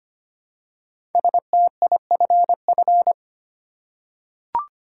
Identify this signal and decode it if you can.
Morse code, heard as a keyed tone, about 25 wpm, 700 Hz. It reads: STIFF
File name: 5 Letter Words - First 3 Letters Given - 25wpm